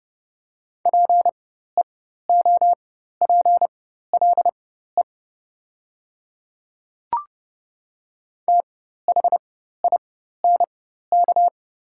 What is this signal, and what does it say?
Morse code, heard as a keyed tone, about 29 wpm, spaced as for 14 wpm, 690 Hz. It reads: PEOPLE THINK
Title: 4x Character Spacing - Top 200 Words - 30wpm